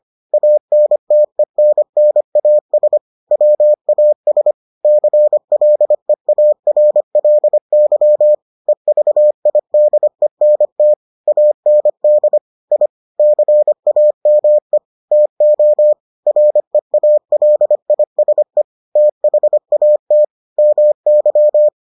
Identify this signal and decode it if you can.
Morse code, heard as a keyed tone, about 25 wpm, 600 Hz. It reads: ANTENNAS WAS CLEARLY EVIDENT AND I CAME TO REALISE THAT MY